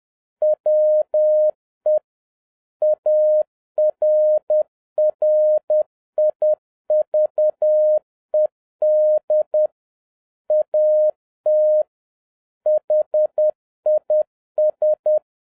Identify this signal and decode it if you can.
Morse code, heard as a keyed tone, about 10 wpm, 610 Hz. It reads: WE ARRIVED AT HIS
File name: 2011-07-mp3 / n46 JOHN K3TN 10wpm